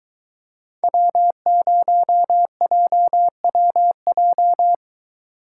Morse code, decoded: W0JWJ

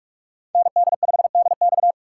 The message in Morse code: ND5DX

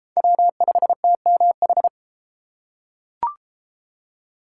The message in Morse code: W5TMH

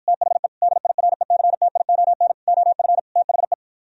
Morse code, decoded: THE DIRECTION OF THE

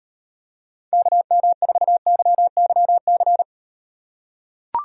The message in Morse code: KM4YYC

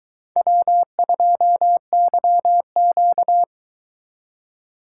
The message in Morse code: W2YQ

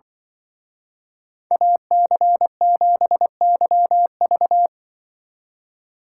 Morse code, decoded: AC7YV

